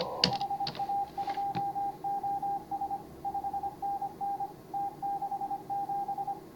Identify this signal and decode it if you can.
Morse code, heard as a keyed tone, about 28 wpm, 810 Hz, with noise. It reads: HA2OS 5NN T67